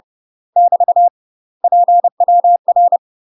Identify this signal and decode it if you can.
Morse code, heard as a keyed tone, about 30 wpm, 700 Hz. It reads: <BT> PWR
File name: UK MASTER QSOs TRAINING / 30WPM / UK MASTER QSOs 0008